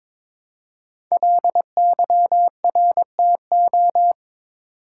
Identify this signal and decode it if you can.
Morse code, heard as a keyed tone, about 22 wpm, 700 Hz. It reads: LYRTO
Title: Code Groups - 5 Letters - 22wpm